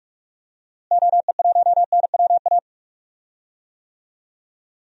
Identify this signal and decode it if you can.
Morse code, heard as a keyed tone, about 45 wpm, 700 Hz. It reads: OE1NWA
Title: DXCC-100 Call Signs - 6 Characters - Repeated 2x - 45wpm